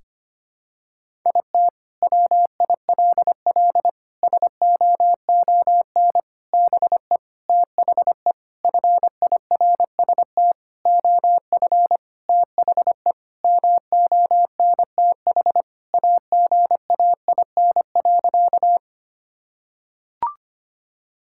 Morse code, heard as a keyed tone, about 25 wpm, 700 Hz. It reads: IT WILL SOON BE THE FIRST OF THE MONTH AGAIN.
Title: Sentences from Top 1000 Words - Review Part 5 - 25wpm